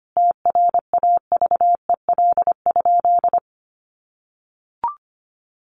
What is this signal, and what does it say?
Morse code, heard as a keyed tone, about 25 wpm, 700 Hz. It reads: TRAVEL?
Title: Sentences from Top 1400 Words 25wpm